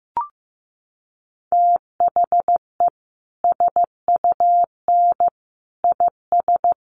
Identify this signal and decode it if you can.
Morse code, heard as a keyed tone, about 15 wpm, 700 Hz. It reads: THE SUN IS
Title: Sentences from Top 1300 Words 15wpm